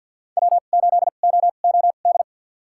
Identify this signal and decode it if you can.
Morse code, heard as a keyed tone, about 51 wpm, spaced as for 36 wpm, 700 Hz. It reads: W8OOD
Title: USA Call Signs - 5 Characters - 2x Character Spacing - 50wpm